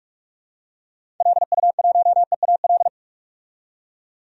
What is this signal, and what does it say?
Morse code, heard as a keyed tone, about 42 wpm, 700 Hz. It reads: LU1EAL
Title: DXCC-100 Call Signs - 6 Characters - 45wpm